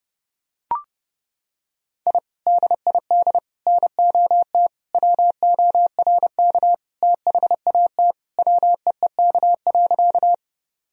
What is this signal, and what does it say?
Morse code, heard as a keyed tone, about 32 wpm, 700 Hz. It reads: I DID NOT WORK THAT WEEK.
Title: Sentences from Top 1000 Words - Review Part 2 - 30wpm